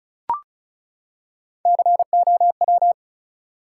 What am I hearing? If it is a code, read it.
Morse code, heard as a keyed tone, about 34 wpm, 700 Hz. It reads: COW
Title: Three Letter Words - 35wpm